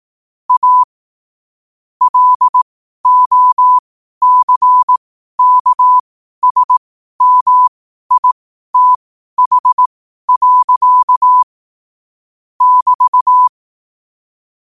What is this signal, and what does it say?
Morse code, heard as a keyed tone, about 18 wpm, spaced as for 12 wpm, 1000 Hz. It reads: A LOCKSMITH. <BT>
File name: qotdfiles / 12wpm